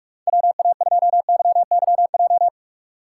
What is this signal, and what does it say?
Morse code, heard as a keyed tone, about 45 wpm, 700 Hz. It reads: WA2YYJ
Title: USA Call Signs - 6 Characters - 3x Character Spacing - 45wpm